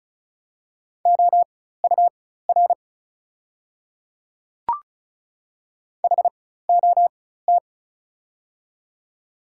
Morse code, heard as a keyed tone, about 36 wpm, spaced as for 16 wpm, 690 Hz. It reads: OUR HOT
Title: 4x Character Spacing - Top 300 Words - 35wpm